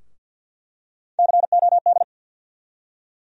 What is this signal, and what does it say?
Morse code, heard as a keyed tone, about 48 wpm, 700 Hz. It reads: COD
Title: USA Call Sign Postfixes - 3 Letters 50wpm